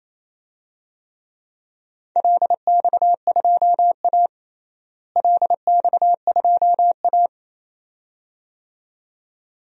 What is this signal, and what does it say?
Morse code, heard as a keyed tone, about 28 wpm, 700 Hz. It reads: LX2A LX2A
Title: DXCC-100 Call Signs - 4 Characters - Repeated 2x - 28wpm